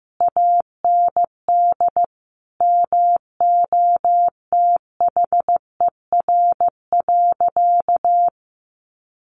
Morse code, heard as a keyed tone, about 15 wpm, 700 Hz. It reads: AND MOTHER.
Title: Sentences from Top 1100 Words 15wpm